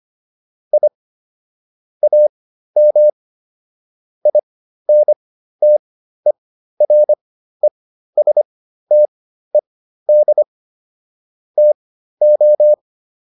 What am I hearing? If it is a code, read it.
Morse code, heard as a keyed tone, about 25 wpm, spaced as for 13 wpm, 600 Hz. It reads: I AM INTERESTED TO
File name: QSO409 13